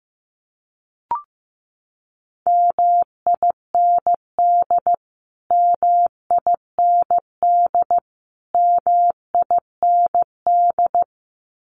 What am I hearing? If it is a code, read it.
Morse code, heard as a keyed tone, about 15 wpm, 700 Hz. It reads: MIND MIND MIND
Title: Top 500 Words Repeated 15wpm